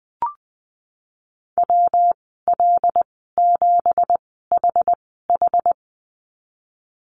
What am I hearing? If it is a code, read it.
Morse code, heard as a keyed tone, about 20 wpm, spaced as for 14 wpm, 700 Hz. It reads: WL7HH
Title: 179 Lesson 29g - Slash - Call Signs 20wpm